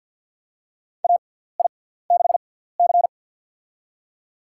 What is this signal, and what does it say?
Morse code, heard as a keyed tone, about 52 wpm, spaced as for 17 wpm, 700 Hz. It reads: AI6C
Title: USA Call Signs - 6x Character Spacing - 50wpm